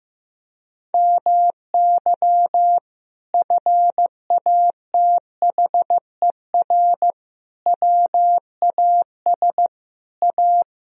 Morse code, heard as a keyed tone, about 15 wpm, 700 Hz. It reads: MY FATHER WAS A
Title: Sentences from Top 1000 Words - Review Part 1 - 15wpm